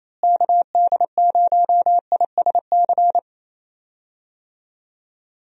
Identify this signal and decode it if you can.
Morse code, heard as a keyed tone, about 27 wpm, 700 Hz. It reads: KD0ISC
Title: USA Call Signs - 6 Characters - Repeated 3x - 28wpm